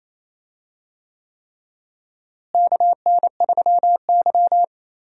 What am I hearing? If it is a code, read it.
Morse code, heard as a keyed tone, about 28 wpm, 700 Hz. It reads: KN3Y
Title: USA Call Signs - 4 Characters - Repeated 3x - 28wpm